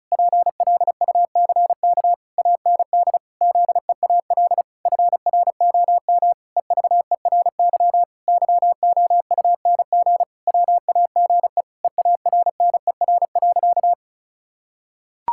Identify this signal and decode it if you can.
Morse code, heard as a keyed tone, about 36 wpm, 700 Hz. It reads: PLUCK AND ZEAL FROM EVERY YOUNG WAGE EARNER.
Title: Pangram Challenge v2 - 35wpm